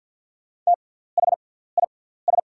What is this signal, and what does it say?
Morse code, heard as a keyed tone, about 48 wpm, spaced as for 17 wpm, 690 Hz. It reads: THIS
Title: 6x Character Spacing - Top 100 Words - 50wpm